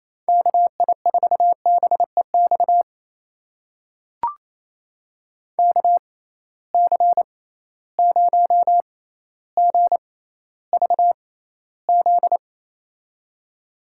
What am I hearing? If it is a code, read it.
Morse code, heard as a keyed tone, about 28 wpm, spaced as for 14 wpm, 700 Hz. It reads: KI4BEX KC0GVZ
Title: USA Call Signs - 6 Characters - 6x Character Spacing - 28wpm